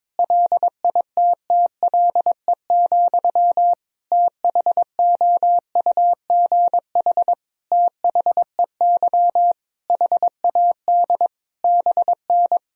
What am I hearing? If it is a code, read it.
Morse code, heard as a keyed tone, about 21 wpm, 700 Hz. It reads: LITTLE, THOUGH THEY HAD BN
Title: Sentences from Top 400 Words and Abbreviations 22wpm